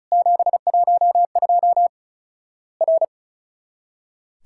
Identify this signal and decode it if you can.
Morse code, heard as a keyed tone, about 34 wpm, 700 Hz. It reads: 712 R
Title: Worked All Europe DX Contest - QTC Exchange - 35wpm